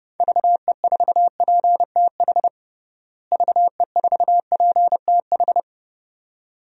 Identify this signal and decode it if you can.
Morse code, heard as a keyed tone, about 30 wpm, 700 Hz. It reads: VE4PTH VE4PTH
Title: DXCC-100 Call Signs - 6 Characters - Repeated 2x - 30wpm